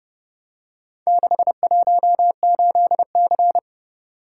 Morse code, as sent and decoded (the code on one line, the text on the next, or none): -.... .---- ---.. -.-.
618C